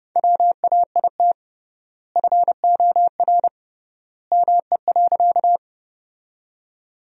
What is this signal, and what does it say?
Morse code, heard as a keyed tone, about 31 wpm, 700 Hz. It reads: WAIT FOR ME.
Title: 3x Word-Spacing - Sentences from Top 500 Words - 30wpm